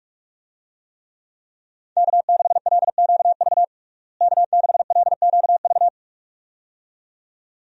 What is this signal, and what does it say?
Morse code, heard as a keyed tone, about 46 wpm, 700 Hz. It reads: K6LQV K6LQV
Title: USA Call Signs - 5 Characters - Repeated 2x - 45wpm